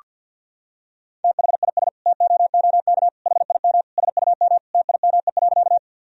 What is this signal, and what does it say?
Morse code, heard as a keyed tone, about 48 wpm, 700 Hz. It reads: THIS TOOK HIM SUM TIME.
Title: Sentences from Top 300 Words and Abbreviations 50wpm